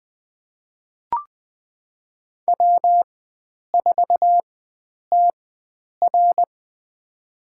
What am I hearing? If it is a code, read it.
Morse code, heard as a keyed tone, about 21 wpm, spaced as for 9 wpm, 700 Hz. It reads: W4TR
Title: USA Call Signs - 4 Characters - 4x Character Spacing - 20wpm